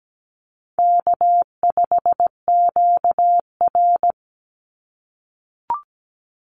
Morse code, heard as a keyed tone, about 17 wpm, 700 Hz. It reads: K5QR
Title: NA SOTA Chaser Callsigns 17wpm